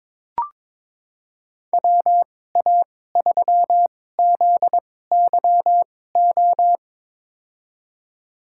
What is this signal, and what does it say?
Morse code, heard as a keyed tone, about 22 wpm, spaced as for 16 wpm, 700 Hz. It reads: WA3ZYO
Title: USA Call Signs - 6 Characters - 2x Character Spacing - 22wpm